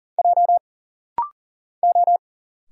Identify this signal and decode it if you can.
Morse code, heard as a keyed tone, about 40 wpm, 700 Hz. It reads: J O
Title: Single Letter-Number Alt - Rapid-Fire 40wpm